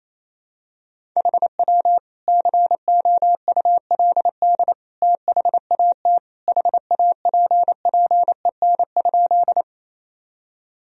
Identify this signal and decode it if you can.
Morse code, heard as a keyed tone, about 28 wpm, 700 Hz. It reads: HW COULD THAT HAPPEN?
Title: Sentences from Top 400 Words and Abbreviations 28wpm